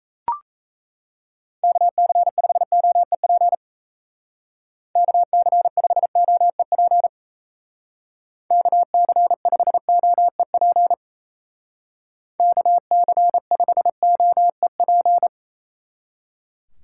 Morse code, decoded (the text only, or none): KC5OEP KC5OEP KC5OEP KC5OEP